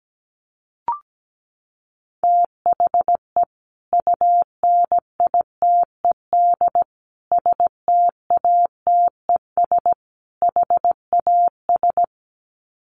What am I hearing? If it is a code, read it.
Morse code, heard as a keyed tone, about 17 wpm, 700 Hz. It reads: THE UNITED STATES HAS
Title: Sets of 4 Words 17wpm